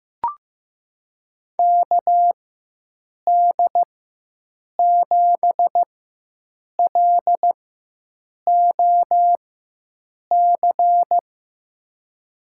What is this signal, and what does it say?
Morse code, heard as a keyed tone, about 15 wpm, spaced as for 7 wpm, 700 Hz. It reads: KD7LOC
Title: USA Call Signs - 6 Characters - 4x Character Spacing - 15wpm